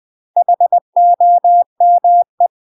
Morse code, heard as a keyed tone, about 20 wpm, 700 Hz. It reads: HOME
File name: UK MASTER QSOs TRAINING / 20WPM / UK MASTER QSOs 0022